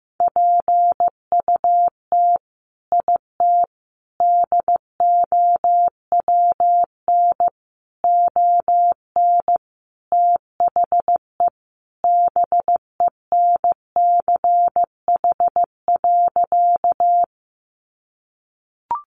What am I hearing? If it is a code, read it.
Morse code, heard as a keyed tone, about 15 wpm, 700 Hz. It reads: PUT IT DOWN ON THE BENCH.
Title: Sentences from Top 1400 Words 15wpm